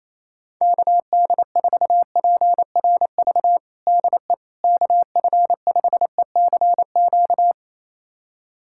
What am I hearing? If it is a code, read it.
Morse code, heard as a keyed tone, about 28 wpm, 700 Hz. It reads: KD4PRV DE KF5ECQ
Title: USA Callsign DE Callsign - 6 Characters - 28wpm